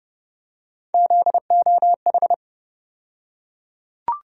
Morse code, heard as a keyed tone, about 30 wpm, 700 Hz. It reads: ZOH
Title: USA Call Sign Postfixes - 3 Letters 30wpm